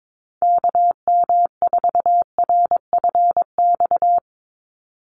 Morse code, heard as a keyed tone, about 22 wpm, 700 Hz. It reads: KM4RFX
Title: USA Call Signs 22wpm